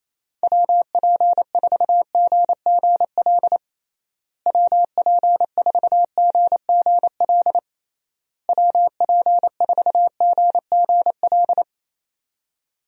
Morse code, heard as a keyed tone, about 27 wpm, 700 Hz. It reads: WP4GGL WP4GGL WP4GGL
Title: USA Call Signs - 6 Characters - Repeated 3x - 28wpm